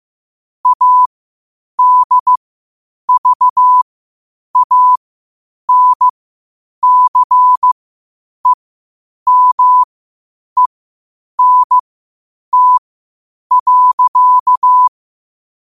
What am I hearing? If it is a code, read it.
Morse code, heard as a keyed tone, about 15 wpm, spaced as for 8 wpm, 1000 Hz. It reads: ADVANCEMENT.
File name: qotdfiles / 8wpm